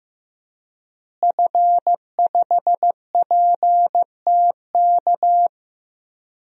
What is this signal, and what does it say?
Morse code, heard as a keyed tone, about 15 wpm, 700 Hz. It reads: F5PTK
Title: DXCC-100 Call Signs - 5 Characters - Repeated 2x - 15wpm